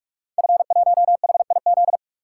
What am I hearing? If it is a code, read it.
Morse code, heard as a keyed tone, about 44 wpm, 700 Hz. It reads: F1HIZ
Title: Code Groups - 5 Letter:Numbers - 45wpm